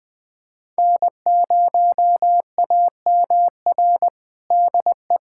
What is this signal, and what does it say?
Morse code, heard as a keyed tone, about 21 wpm, 700 Hz. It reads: N0AMR DE
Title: USA Callsign DE Callsign - 5 Characters - 20wpm